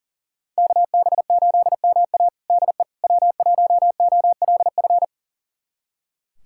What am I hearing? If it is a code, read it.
Morse code, heard as a keyed tone, about 40 wpm, 700 Hz. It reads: KB8MA DE W1OLF
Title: USA Callsign DE Callsign - 5 Characters - 40wpm